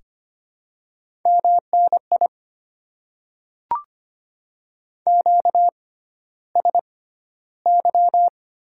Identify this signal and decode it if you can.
Morse code, heard as a keyed tone, about 25 wpm, spaced as for 13 wpm, 700 Hz. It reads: MNI QSY
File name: Top 90 QSO Elements - 6x Character Spacing - 25wpm